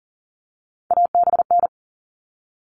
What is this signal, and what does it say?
Morse code, heard as a keyed tone, about 40 wpm, 700 Hz. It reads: ABN
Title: All Cut Numbers - 3-Digit 40wpm